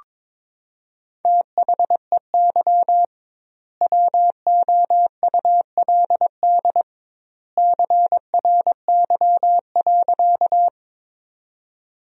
Morse code, heard as a keyed tone, about 21 wpm, 700 Hz. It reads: THEY WOULD CRY.
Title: 2x Word-Spacing - Sentences from Top 500 Words - 22wpm